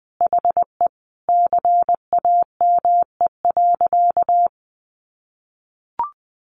Morse code, heard as a keyed tone, about 20 wpm, 700 Hz. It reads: HE CAME.
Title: Sentences from Top 1500 Words 20wpm